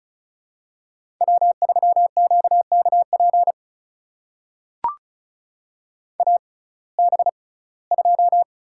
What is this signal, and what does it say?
Morse code, heard as a keyed tone, about 36 wpm, spaced as for 21 wpm, 700 Hz. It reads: W3QKP AB2
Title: USA Call Signs - 5 Characters - 6x Character Spacing - 35wpm